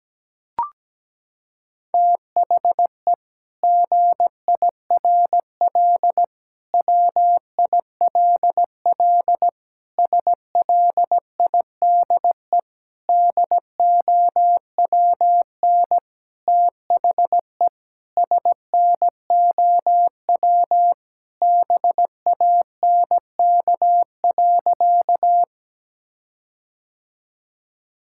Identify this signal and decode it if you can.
Morse code, heard as a keyed tone, about 17 wpm, 700 Hz. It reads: THE GIRL WILL SLIDE DOWN THE SNOW BANK.
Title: Sentences from Top 1600 Words 17wpm